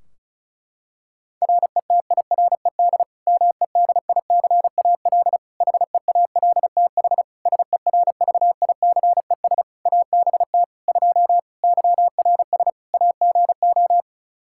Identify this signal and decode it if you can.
Morse code, heard as a keyed tone, about 36 wpm, 700 Hz. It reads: RETIRED MEDICAL HEALTH SERVICES ABT 2 YRS AGO
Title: Practice QSO - Exchange 3 - More Info - 35wpm